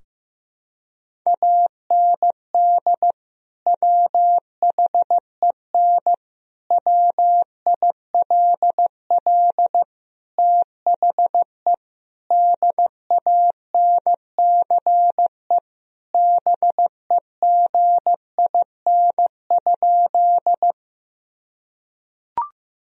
Morse code, heard as a keyed tone, about 15 wpm, 700 Hz. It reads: AND WHEN WILL THE DANCE BEGIN?
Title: Sentences from Top 1000 Words - Review Part 4 - 15wpm